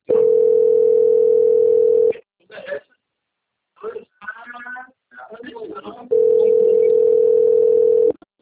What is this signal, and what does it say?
Ringback tone (North America), 440 + 480 Hz, with speech.
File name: out- - - - - .3099